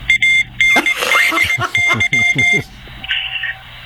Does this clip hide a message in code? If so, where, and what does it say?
Yes: Morse code from 0.1 to 2.6 s, heard as a keyed tone, about 19 wpm, 2100 Hz, with speech. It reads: ANDY